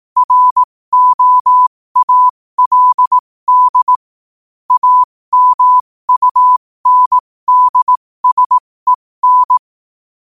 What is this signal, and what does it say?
Morse code, heard as a keyed tone, about 18 wpm, spaced as for 15 wpm, 1000 Hz. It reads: ROALD AMUNDSEN
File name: 15wpm